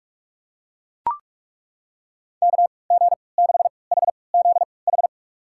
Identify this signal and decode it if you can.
Morse code, heard as a keyed tone, about 42 wpm, spaced as for 25 wpm, 700 Hz. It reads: KG6HZH
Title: USA Call Signs - 3x Character Spacing - 45wpm